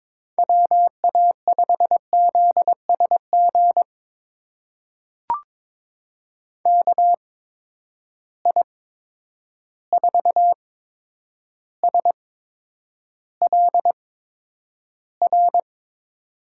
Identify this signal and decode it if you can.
Morse code, heard as a keyed tone, about 22 wpm, spaced as for 9 wpm, 700 Hz. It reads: WA5ZSG KI4SLR
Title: USA Call Signs - 6 Characters - 8x Character Spacing - 22wpm